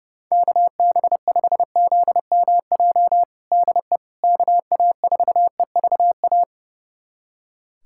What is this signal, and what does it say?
Morse code, heard as a keyed tone, about 30 wpm, 700 Hz. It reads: KB5ZMJ DE KA4EVA
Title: USA Callsign DE Callsign - 6 Characters - 30wpm